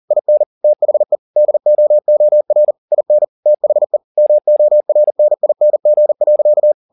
Morse code, heard as a keyed tone, about 40 wpm, 600 Hz. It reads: IN THE DOOR IN THE MORNING.
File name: the old man and the sea 2 40wpm